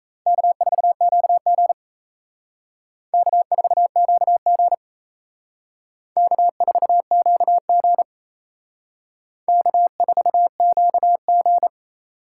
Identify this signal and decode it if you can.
Morse code, heard as a keyed tone, about 35 wpm, 700 Hz. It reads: K4QG K4QG K4QG K4QG